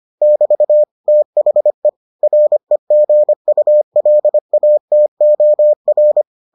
Morse code, heard as a keyed tone, about 25 wpm, 600 Hz. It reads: <BT> THE REGULATOR